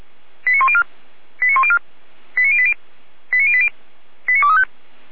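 Five-tone selective call (ZVEI), heard as quick runs of five tones, tones of about 70 ms. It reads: 89104 89104 89080 89080 89235
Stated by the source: Five-tone ZVEI selective call received by SDR.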